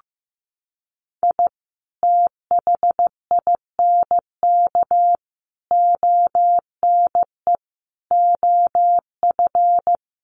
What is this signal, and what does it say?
Morse code, heard as a keyed tone, about 15 wpm, 700 Hz. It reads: I THINK ONE OF
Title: Sets of 5 Words 15wpm